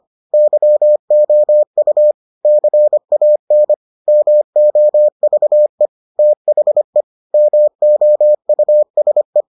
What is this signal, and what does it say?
Morse code, heard as a keyed tone, about 25 wpm, 600 Hz. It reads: YOU CAN MOVE THE MOUSE